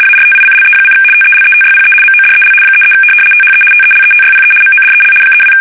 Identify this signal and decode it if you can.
RTTY, about 75 baud. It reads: ARTICLE: TELETYPE MODEL 33 THE TELETYPE MODEL 32
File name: fsk850-75B / teletype-corp